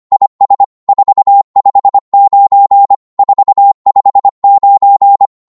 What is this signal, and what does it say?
Morse code, heard as a keyed tone, about 25 wpm, 800 Hz. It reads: IS 459 459
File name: Morse / qso-25wpm